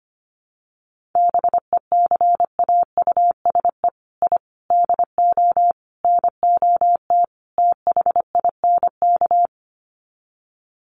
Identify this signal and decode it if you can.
Morse code, heard as a keyed tone, about 25 wpm, 700 Hz. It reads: BECAUSE I DO NOT THINK
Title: Sets of 5 Words Encore 25wpm